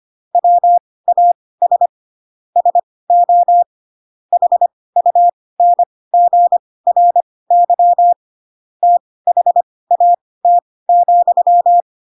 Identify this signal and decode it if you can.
Morse code, heard as a keyed tone, about 25 wpm, spaced as for 18 wpm, 700 Hz. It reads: WAS SO HUNGRY THAT,
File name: Pinocchio / Pinocchio ch6 12 25 ik1hge